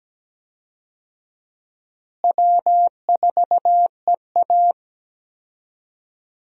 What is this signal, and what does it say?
Morse code, heard as a keyed tone, about 17 wpm, 700 Hz. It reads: W4EA
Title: USA Call Signs - 4 Characters - Repeated 3x - 17wpm